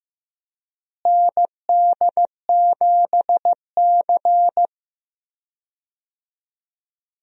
Morse code, heard as a keyed tone, about 15 wpm, 700 Hz. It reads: ND7C
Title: USA Call Signs - 4 Characters - Repeated 3x - 15wpm